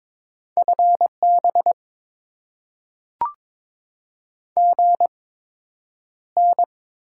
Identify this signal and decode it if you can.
Morse code, heard as a keyed tone, about 22 wpm, spaced as for 9 wpm, 700 Hz. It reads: FB GN
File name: Top 90 QSO Elements - 8x Character Spacing - 22wpm